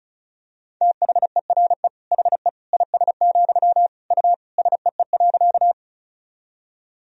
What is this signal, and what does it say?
Morse code, heard as a keyed tone, about 36 wpm, 700 Hz. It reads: THERE HE IS, U SEE.
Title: Sentences from Top 100 Words and Abbreviations 35wpm